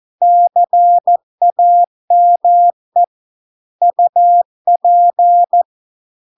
Morse code, heard as a keyed tone, about 14 wpm, 700 Hz. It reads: CAME UP